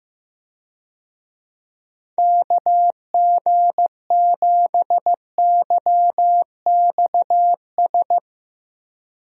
Morse code, heard as a keyed tone, about 15 wpm, 700 Hz. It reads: KG7YXS
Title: USA Call Signs - 6 Characters - Repeated 3x - 15wpm